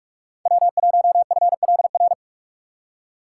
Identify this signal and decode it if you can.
Morse code, heard as a keyed tone, about 44 wpm, 700 Hz. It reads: W1FLR
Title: USA Call Signs - 3x Character Spacing - 45wpm